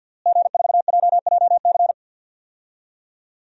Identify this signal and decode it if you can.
Morse code, heard as a keyed tone, about 48 wpm, 700 Hz. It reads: G4JJC
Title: DXCC-100 Call Signs - 5 Characters - Repeated 2x - 50wpm